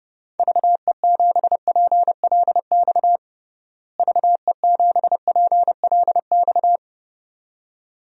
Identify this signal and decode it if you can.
Morse code, heard as a keyed tone, about 30 wpm, 700 Hz. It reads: VE7PLX VE7PLX
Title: DXCC-100 Call Signs - 6 Characters - Repeated 2x - 30wpm